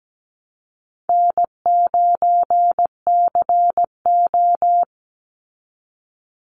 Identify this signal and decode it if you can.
Morse code, heard as a keyed tone, about 17 wpm, 700 Hz. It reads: N9CO
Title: Easy Call Signs Repeated 2x - 17wpm